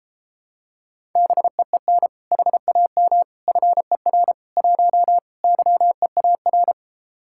Morse code, heard as a keyed tone, about 33 wpm, 700 Hz. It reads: BEEN HAM FER 1 YEAR